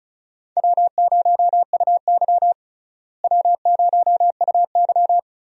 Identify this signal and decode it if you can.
Morse code, heard as a keyed tone, about 36 wpm, 700 Hz. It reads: W0UY W0UY
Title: USA Call Signs - 4 Characters - Repeated 3x - 35wpm